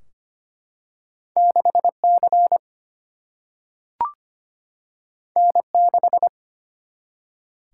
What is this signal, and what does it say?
Morse code, heard as a keyed tone, about 25 wpm, 700 Hz. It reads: 6C N6
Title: Single-Letter Single-Number 25wpm